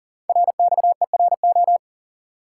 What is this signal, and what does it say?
Morse code, heard as a keyed tone, about 41 wpm, 700 Hz. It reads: RXERO
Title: Code Groups - 5 Letters - 40wpm